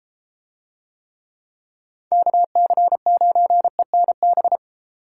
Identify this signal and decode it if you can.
Morse code, heard as a keyed tone, about 31 wpm, 700 Hz. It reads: KC9ENB